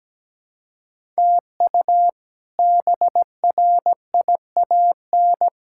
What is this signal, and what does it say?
Morse code, heard as a keyed tone, about 17 wpm, 700 Hz. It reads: TU BRIAN
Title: K1USN SST QSO Practice - 17wpm